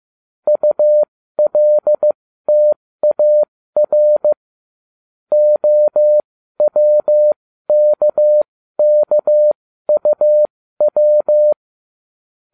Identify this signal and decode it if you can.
Morse code, heard as a keyed tone, about 15 wpm, spaced as for 13 wpm, 600 Hz. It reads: ULTAR OWKKUW